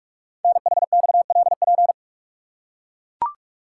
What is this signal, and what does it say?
Morse code, heard as a keyed tone, about 44 wpm, 700 Hz. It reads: NHXLP
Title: Code Groups - 5 Letter:Numbers - 45wpm